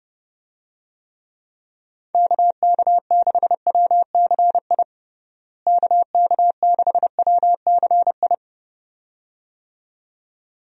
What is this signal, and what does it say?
Morse code, heard as a keyed tone, about 32 wpm, 700 Hz. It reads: KK6WCI KK6WCI
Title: USA Call Signs - 6 Characters - Repeated 2x - 30wpm